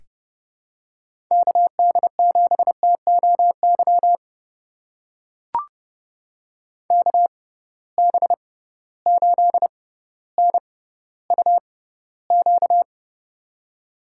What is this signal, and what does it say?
Morse code, heard as a keyed tone, about 30 wpm, spaced as for 15 wpm, 700 Hz. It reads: KD7TOY KB8NUQ
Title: USA Call Signs - 6 Characters - 6x Character Spacing - 30wpm